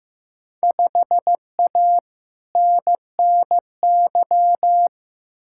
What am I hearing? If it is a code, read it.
Morse code, heard as a keyed tone, about 15 wpm, 700 Hz. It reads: 5A NNY
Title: Field Day Exchanges - 15wpm